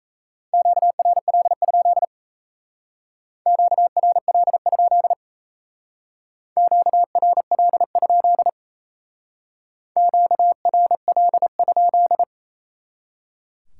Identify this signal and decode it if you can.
Morse code, heard as a keyed tone, about 36 wpm, 700 Hz. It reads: QRL? QRL? QRL? QRL?